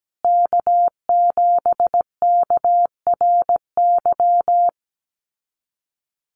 Morse code, heard as a keyed tone, about 17 wpm, 700 Hz. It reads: K7KRY
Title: USA Call Signs Repeated 2x 17wpm